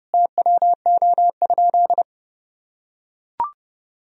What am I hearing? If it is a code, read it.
Morse code, heard as a keyed tone, about 30 wpm, 700 Hz. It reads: TWO?
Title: Sentences from Top 1000 Words - Review Part 2 - 30wpm